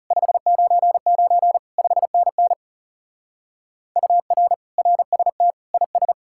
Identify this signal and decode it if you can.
Morse code, heard as a keyed tone, about 35 wpm, 700 Hz. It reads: 599 5NN UR RST IS
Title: Speed-Racing Common QSO Phrases - 30wpm